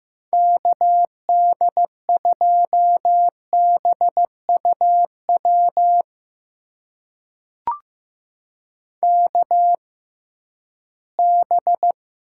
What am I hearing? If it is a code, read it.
Morse code, heard as a keyed tone, about 15 wpm, spaced as for 11 wpm, 700 Hz. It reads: KD2BUW KB
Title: USA Call Signs - 6 Characters - 6x Character Spacing - 15wpm